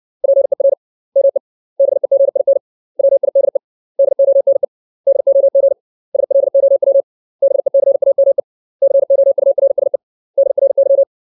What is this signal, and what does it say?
Morse code, heard as a keyed tone, about 60 wpm, 550 Hz. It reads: PER ME BEGIN WIDE DONE DOG SLOW BECAME COURSE DRY
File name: 632 words / 632 Words 60WPM